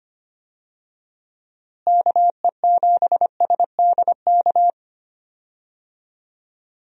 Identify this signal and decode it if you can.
Morse code, heard as a keyed tone, about 25 wpm, 700 Hz. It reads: KE7SDK
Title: USA Call Signs - 6 Characters - Repeated 2x - 25wpm